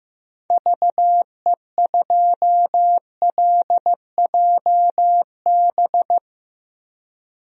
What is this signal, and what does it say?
Morse code, heard as a keyed tone, about 15 wpm, 700 Hz. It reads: VE2LJB
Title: DXCC-100 Call Signs - 6 Characters - Repeated 2x - 15wpm